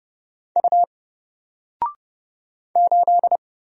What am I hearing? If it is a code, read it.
Morse code, heard as a keyed tone, about 30 wpm, 700 Hz. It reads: U 8
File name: Single-Letter-Number-30wpm